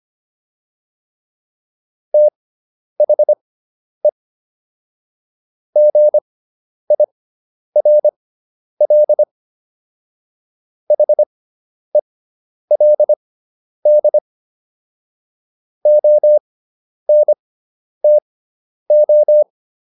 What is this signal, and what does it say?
Morse code, heard as a keyed tone, about 25 wpm, spaced as for 10 wpm, 600 Hz. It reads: THE GIRL HELD ONTO